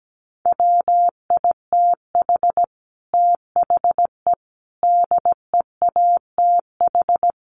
Morse code, heard as a keyed tone, about 17 wpm, 700 Hz. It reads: WITH THE DEATH
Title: A Sound Of Thunder 17wpm